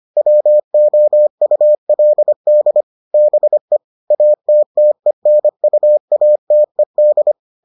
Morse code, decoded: WOULD BE ATTENUATED

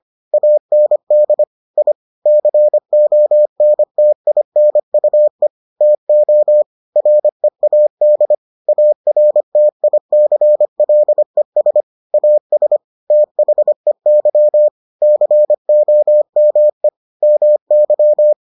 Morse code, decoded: AND I CONTINUE TO READ ARTICLES AS THEY COME MY

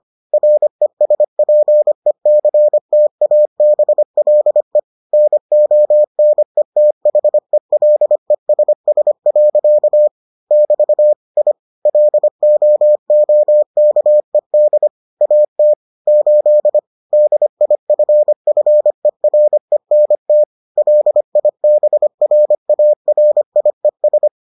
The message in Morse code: RESPECTABLE NONETHELESS. <BT> I LOOKED AT 8 DIFFERENT LIBRARIES

